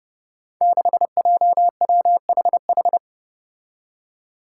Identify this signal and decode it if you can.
Morse code, heard as a keyed tone, about 30 wpm, 700 Hz. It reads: 6JWHH
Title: Five Challenging Letter-Number Combinations 30wpm